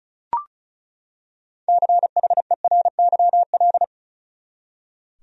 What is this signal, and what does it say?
Morse code, heard as a keyed tone, about 36 wpm, 700 Hz. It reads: CHERYL
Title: USA Common Names - 35wpm